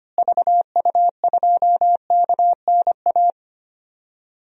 Morse code, heard as a keyed tone, about 25 wpm, 700 Hz. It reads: VU2KNA
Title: DXCC-100 Call Signs - 6 Characters - Repeated 2x - 25wpm